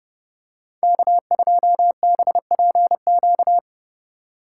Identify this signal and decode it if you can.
Morse code, heard as a keyed tone, about 30 wpm, 700 Hz. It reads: K2BPQ
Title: USA Call Signs - 5 Characters - 2x Character Spacing - 30wpm